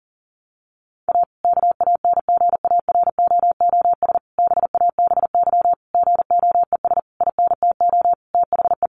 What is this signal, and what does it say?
Morse code, heard as a keyed tone, about 40 wpm, 700 Hz. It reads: A KANGAROOS BABY GOES INTO THE
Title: Easy Phrases 40wpm